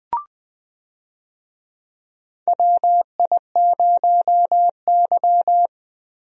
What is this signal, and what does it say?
Morse code, heard as a keyed tone, about 20 wpm, 700 Hz. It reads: WI0Y
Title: USA Call Signs - 4 Characters - Repeated 2x - 20wpm